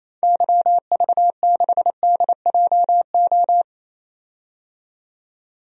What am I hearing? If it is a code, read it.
Morse code, heard as a keyed tone, about 28 wpm, 700 Hz. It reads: YV6DJO
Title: DXCC-100 Call Signs - 6 Characters - 28wpm